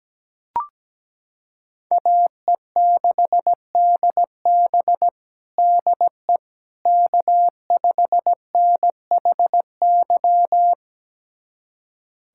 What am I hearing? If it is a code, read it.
Morse code, heard as a keyed tone, about 17 wpm, 700 Hz. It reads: AE6DB DE K5NHY
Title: USA Callsign DE Callsign - 5 Characters - 17wpm